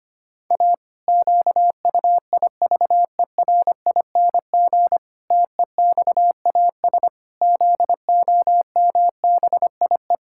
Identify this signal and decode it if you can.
Morse code, heard as a keyed tone, about 25 wpm, 700 Hz. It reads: A QUIVERING TEXAS ZOMBIE
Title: Pangram Challenge v2 - 25wpm